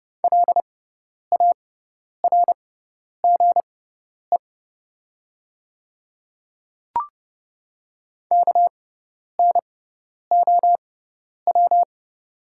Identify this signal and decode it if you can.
Morse code, heard as a keyed tone, about 32 wpm, spaced as for 10 wpm, 700 Hz. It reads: LARGE KNOW
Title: 6x Character Spacing - Top 300 Words - 30wpm